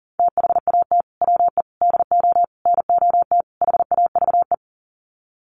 Morse code, heard as a keyed tone, about 37 wpm, 700 Hz. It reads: THAT WE DO NOT HAVE
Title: Sets of 5 Words Encore 40wpm